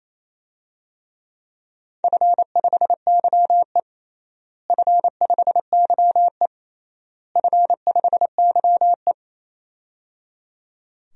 Morse code, heard as a keyed tone, about 28 wpm, 700 Hz. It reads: F5YE F5YE F5YE
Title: DXCC-100 Call Signs - 4 Characters - Repeated 3x - 28wpm